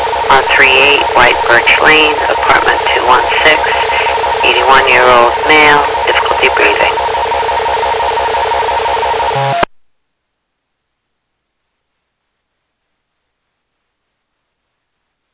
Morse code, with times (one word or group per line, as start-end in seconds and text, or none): none